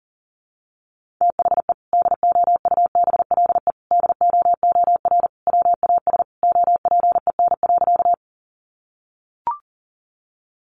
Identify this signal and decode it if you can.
Morse code, heard as a keyed tone, about 40 wpm, 700 Hz. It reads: THE DOUBLE DOOR WAS OPEN.
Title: Sentences from Top 1400 Words 40wpm